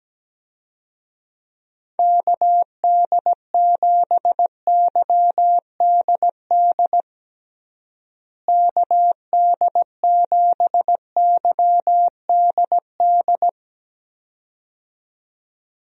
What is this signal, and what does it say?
Morse code, heard as a keyed tone, about 17 wpm, 700 Hz. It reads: KD7YDD KD7YDD
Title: USA Call Signs - 6 Characters - Repeated 2x - 17wpm